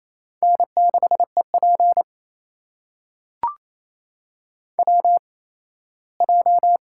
Morse code, heard as a keyed tone, about 28 wpm, spaced as for 17 wpm, 700 Hz. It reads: N6EP WJ
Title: USA Call Signs - 8x Character Spacing - 28wpm